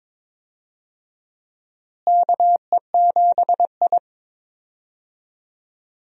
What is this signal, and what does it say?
Morse code, heard as a keyed tone, about 22 wpm, 700 Hz. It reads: KE7I
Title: USA Call Signs - 4 Characters - Repeated 2x - 22wpm